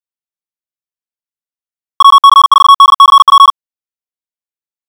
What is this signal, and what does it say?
Morse code, heard as a keyed tone, about 56 wpm, 1110 Hz. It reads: WC4DCP
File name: ja7uhv / SampleHST2